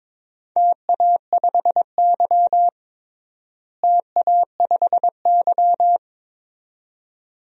Morse code, heard as a keyed tone, about 22 wpm, 700 Hz. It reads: TA5Y TA5Y
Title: DXCC-100 Call Signs - 4 Characters - Repeated 2x - 22wpm